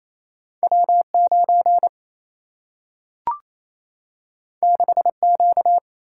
Morse code, heard as a keyed tone, about 28 wpm, 700 Hz. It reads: W9 6Q
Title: Single-Letter Single-Number 28wpm